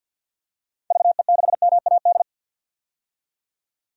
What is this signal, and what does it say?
Morse code, heard as a keyed tone, about 50 wpm, 700 Hz. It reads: VE6MAD
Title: DXCC-100 Call Signs - 6 Characters - 50wpm